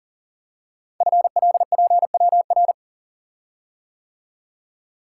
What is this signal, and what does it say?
Morse code, heard as a keyed tone, about 44 wpm, 700 Hz. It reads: FLPWR
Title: Code Groups - 5 Letters - 40wpm